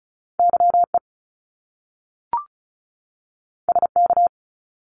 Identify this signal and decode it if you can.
Morse code, heard as a keyed tone, about 36 wpm, 700 Hz. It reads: YE SK
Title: Top Two Letter Combinations in Words 35wpm